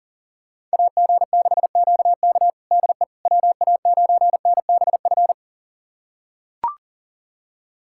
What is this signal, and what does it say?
Morse code, heard as a keyed tone, about 40 wpm, 700 Hz. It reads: AG6QK DE WA9NBF
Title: USA Callsign DE Callsign 40wpm